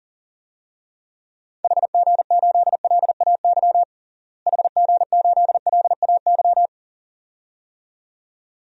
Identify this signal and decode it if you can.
Morse code, heard as a keyed tone, about 40 wpm, 700 Hz. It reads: HG8LAY HG8LAY
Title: DXCC-100 Call Signs - 6 Characters - Repeated 2x - 40wpm